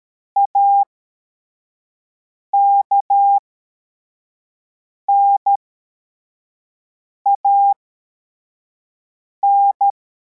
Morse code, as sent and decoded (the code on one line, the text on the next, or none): .- -.- -. .- -.
AKNAN